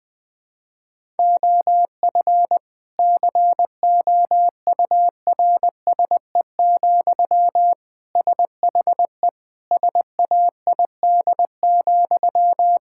Morse code, heard as a keyed tone, about 21 wpm, 700 Hz. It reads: OF COURSE, SHE SAID,
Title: Sentences from Top 1600 Words 20wpm